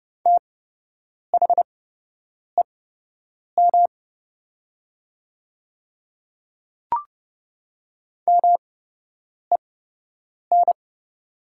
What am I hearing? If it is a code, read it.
Morse code, heard as a keyed tone, about 30 wpm, spaced as for 8 wpm, 700 Hz. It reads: THEM MEN
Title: 8x Character Spacing - Top 200 Words - 30wpm